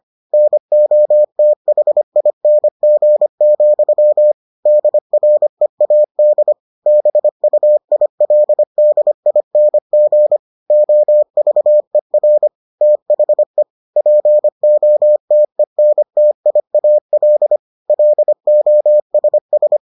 Morse code, heard as a keyed tone, about 25 wpm, 600 Hz. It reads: NOTHING, DREAD BUILDING OVER THE POTENTIAL LOSS